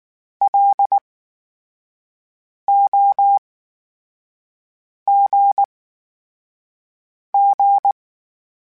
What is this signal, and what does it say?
Morse code, heard as a keyed tone, about 19 wpm, spaced as for 5 wpm, 800 Hz. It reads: LOGG
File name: Special INGs / Special INGs-mp3s / VerySlowToFaster Ing @18